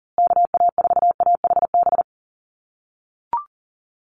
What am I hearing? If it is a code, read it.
Morse code, heard as a keyed tone, about 40 wpm, 700 Hz. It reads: KA4AHB
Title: USA Call Signs Repeated 2x 40wpm